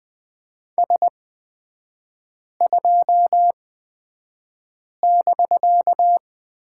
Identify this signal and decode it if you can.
Morse code, heard as a keyed tone, about 20 wpm, spaced as for 5 wpm, 700 Hz. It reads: S2<BK>
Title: 250 Lesson 40e - BK - Three Characters 20wpm